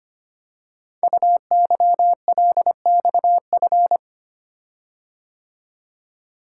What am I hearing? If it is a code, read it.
Morse code, heard as a keyed tone, about 25 wpm, 700 Hz. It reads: UYLXF
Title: Code Groups - 5 Letters - 25wpm